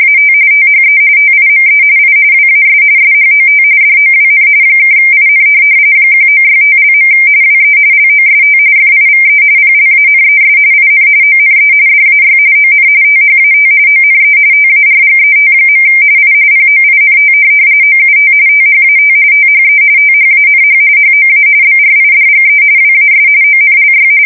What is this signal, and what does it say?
RTTY, about 45.45 baud. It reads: AT THE TIME I WAS SHOT DOWN (IN 1965). WE LOST FIVE PLANES IN ONE DAY ON THAT ONE TARGET. I POINT OUT THAT THE THANH HOA BRIDGE WAS ONLY
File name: fsk170 / USS-New-Jersey